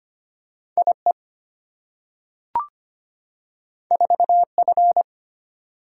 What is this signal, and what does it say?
Morse code, heard as a keyed tone, about 25 wpm, 700 Hz. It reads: IE 4F
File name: Two Letter-Number Combinations - 25wpm